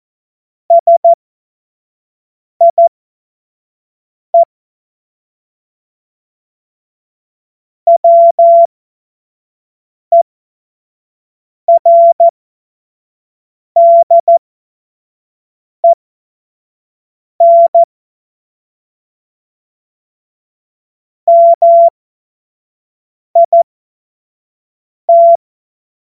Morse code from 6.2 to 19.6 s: WERDEN